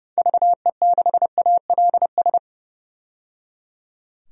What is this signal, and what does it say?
Morse code, heard as a keyed tone, about 30 wpm, 700 Hz. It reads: VE6ALS
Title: DXCC-100 Call Signs - 6 Characters - 30wpm